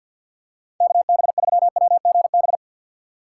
Morse code, heard as a keyed tone, about 52 wpm, 700 Hz. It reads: KB3WGB
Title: USA Call Signs - 6 Characters - 6x Character Spacing - 50wpm